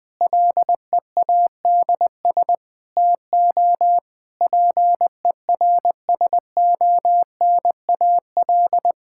Morse code, heard as a keyed tone, about 21 wpm, 700 Hz. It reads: LEADS TO PERSONAL
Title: Difficult Sentences with Hints - 20wpm